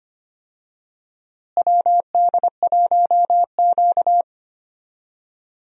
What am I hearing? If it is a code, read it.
Morse code, heard as a keyed tone, about 25 wpm, 700 Hz. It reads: WD1Q
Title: USA Call Signs - 4 Characters - Repeated 2x - 25wpm